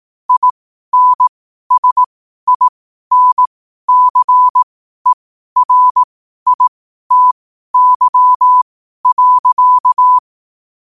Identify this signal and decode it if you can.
Morse code, heard as a keyed tone, about 18 wpm, spaced as for 12 wpm, 1000 Hz. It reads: INSINCERITY.
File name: qotdfiles / 12wpm